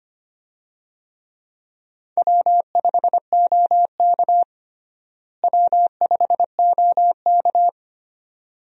Morse code, heard as a keyed tone, about 25 wpm, 700 Hz. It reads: W5OK W5OK
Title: USA Call Signs - 4 Characters - Repeated 2x - 25wpm